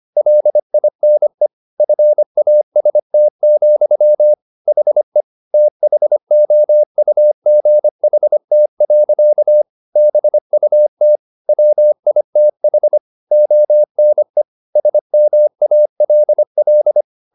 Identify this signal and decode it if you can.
Morse code, heard as a keyed tone, about 25 wpm, 600 Hz. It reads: LINE FAST, HE THOUGHT. BUT WITH ONE SMALL